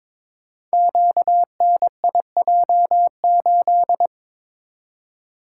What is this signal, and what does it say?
Morse code, heard as a keyed tone, about 22 wpm, 700 Hz. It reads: QNIJ8
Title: Code Groups - 5 Letter:Numbers - 22wpm